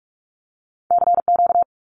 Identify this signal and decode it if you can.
Morse code, heard as a keyed tone, about 44 wpm, 700 Hz. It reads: CQ
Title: Abbreviations 45wpm